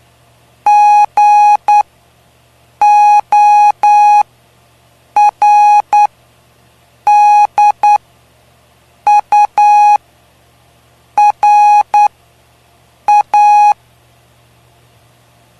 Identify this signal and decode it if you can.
Morse code, heard as a keyed tone, about 9 wpm, spaced as for 6 wpm, 830 Hz. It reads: GORDURA
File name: cw / py2lcd / 05-PY2LCD-Li��o 5